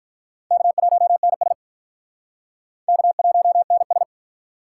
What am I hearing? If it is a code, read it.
Morse code, heard as a keyed tone, about 46 wpm, 700 Hz. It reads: K1NS K1NS